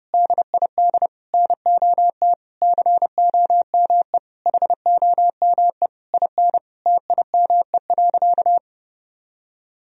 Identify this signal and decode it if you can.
Morse code, heard as a keyed tone, about 30 wpm, 700 Hz. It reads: DID NOT COME HOME IN TIME.
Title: Sentences from Top 1000 Words - Review Part 3 - 30wpm